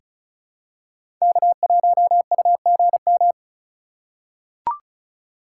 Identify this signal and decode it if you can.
Morse code, heard as a keyed tone, about 36 wpm, 700 Hz. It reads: K1UGM
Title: USA Call Signs - 5 Characters - 2x Character Spacing - 35wpm